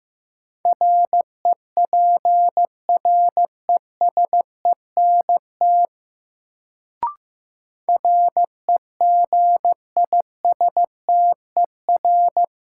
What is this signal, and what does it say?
Morse code, heard as a keyed tone, about 15 wpm, 700 Hz. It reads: REPRESENT REGISTER
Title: Sentences from Top 1200 Words - New Word Drill Alt - 15wpm